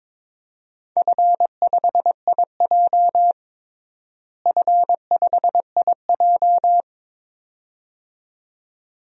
Morse code, heard as a keyed tone, about 21 wpm, 700 Hz. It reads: F5IJ F5IJ
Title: DXCC-100 Call Signs - 4 Characters - Repeated 2x - 22wpm